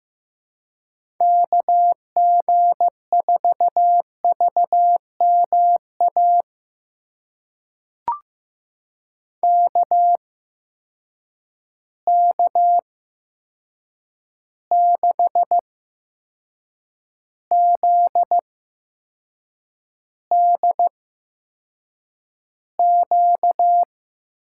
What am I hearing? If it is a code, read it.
Morse code, heard as a keyed tone, about 15 wpm, spaced as for 6 wpm, 700 Hz. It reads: KG4VMA KK6ZDQ
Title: USA Call Signs - 6 Characters - 8x Character Spacing - 15wpm